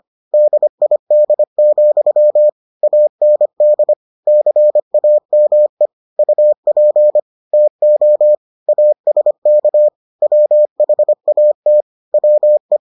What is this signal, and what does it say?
Morse code, heard as a keyed tone, about 25 wpm, 600 Hz. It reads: DID, AND CAME UP TO ASK WHAT WE